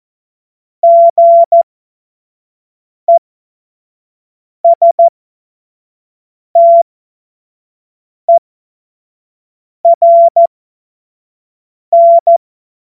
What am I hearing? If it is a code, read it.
Morse code, heard as a keyed tone, about 14 wpm, spaced as for 5 wpm, 680 Hz. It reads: GESTERN
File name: Morsekurs / Witze / Witz 18. Januar 2022, 25 F70